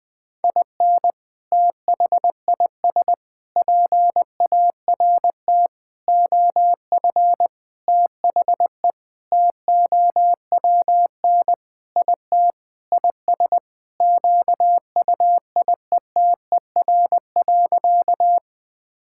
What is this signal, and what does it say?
Morse code, heard as a keyed tone, about 20 wpm, 700 Hz. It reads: IN THIS PART OF THE TOWN IT IS QUIETER.
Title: Sentences with 5-Word Phrases - 20wpm